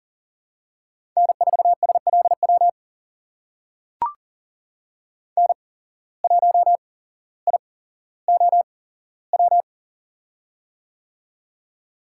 Morse code, heard as a keyed tone, about 40 wpm, spaced as for 17 wpm, 700 Hz. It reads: N4SLW N1IOW
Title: USA Call Signs - 5 Characters - 8x Character Spacing - 40wpm